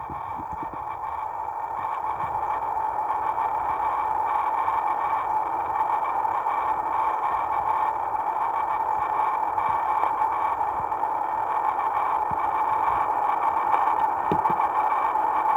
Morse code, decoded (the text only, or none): none